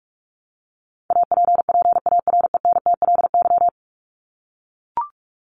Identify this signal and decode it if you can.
Morse code, heard as a keyed tone, about 44 wpm, 700 Hz. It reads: APPARENTLY
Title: LY Suffix - Top 100 Words with Base Words 45wpm